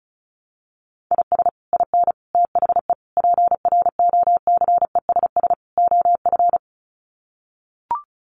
Morse code, heard as a keyed tone, about 34 wpm, 700 Hz. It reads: IS IN THE PROCESS OF
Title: Sets of 5 Words Encore2 35wpm